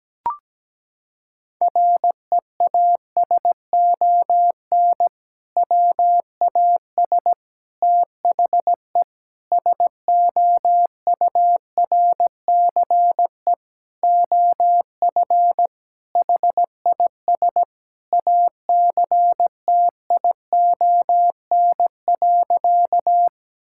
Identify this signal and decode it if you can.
Morse code, heard as a keyed tone, about 17 wpm, 700 Hz. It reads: REASON WAS THE SOURCE OF HIS ACTION.
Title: Sentences from Top 1000 Words - Review Part 5 - 17wpm